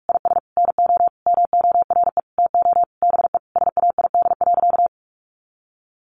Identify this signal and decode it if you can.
Morse code, heard as a keyed tone, about 44 wpm, 700 Hz. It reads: IS NO MORE TO BE SAID.
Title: Sentences from Top 100 Words 45wpm